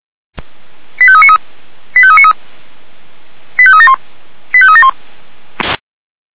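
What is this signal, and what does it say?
Five-tone selective call (ZVEI), heard as quick runs of five tones, tones of about 70 ms. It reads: 85393 85393 85371 85371
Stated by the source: ZVEI five-tone call received via SDR.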